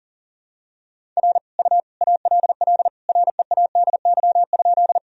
Morse code, heard as a keyed tone, about 40 wpm, 700 Hz. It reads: R U ALL READY?
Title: Sentences from Top 400 Words and Abbreviations 40wpm